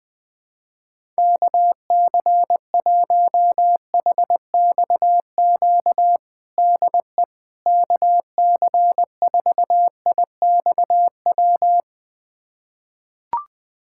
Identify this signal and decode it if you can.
Morse code, heard as a keyed tone, about 21 wpm, 700 Hz. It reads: KC1HXQ DE KC4IXW
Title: USA Callsign DE Callsign - 6 Characters - 20wpm